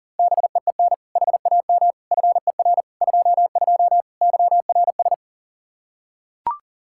Morse code, heard as a keyed tone, about 37 wpm, 700 Hz. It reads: BEEN HAM FER 22 YRS
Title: Practice QSO - Exchange 3 - More Info - 40wpm